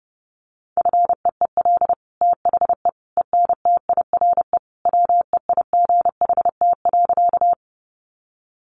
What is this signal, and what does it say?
Morse code, heard as a keyed tone, about 30 wpm, 700 Hz. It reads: FEEL THE ENTIRE WEIGHT.
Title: Sentences from Top 800 Words 30wpm